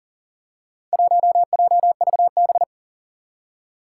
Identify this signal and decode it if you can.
Morse code, heard as a keyed tone, about 40 wpm, 700 Hz. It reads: 1JVB
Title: Four Challenging Letter-Number Combinations 40wpm